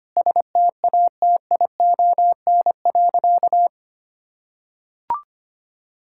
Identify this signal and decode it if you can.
Morse code, heard as a keyed tone, about 25 wpm, 700 Hz. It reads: STATION.
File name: practice / 25wpm / Difficult Sentences with Hints - 25wpm